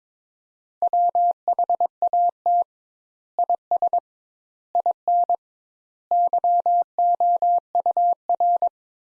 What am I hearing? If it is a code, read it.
Morse code, heard as a keyed tone, about 22 wpm, 700 Hz. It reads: WHAT IS IN YOUR
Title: Sentences Top 200 Words - 22wpm